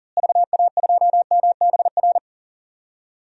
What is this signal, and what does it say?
Morse code, heard as a keyed tone, about 40 wpm, 700 Hz. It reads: VA2MBR
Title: DXCC-100 Call Signs - 6 Characters - Repeated 3x - 40wpm